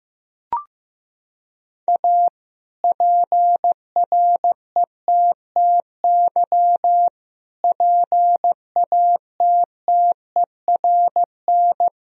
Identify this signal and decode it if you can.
Morse code, heard as a keyed tone, about 15 wpm, 700 Hz. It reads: A PRETTY PATTERN
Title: Sentences from Top 1000 Words - Review Part 2 - 15wpm